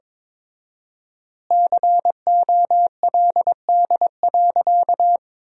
Morse code, heard as a keyed tone, about 21 wpm, 700 Hz. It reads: COLD.
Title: 4x Word-Spacing - Sentences from Top 400 Words - 22wpm